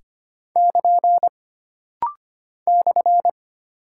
Morse code, heard as a keyed tone, about 25 wpm, 700 Hz. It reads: <KN> /
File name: Single Punctuation-Prosign - Rapid-Fire - 25wpm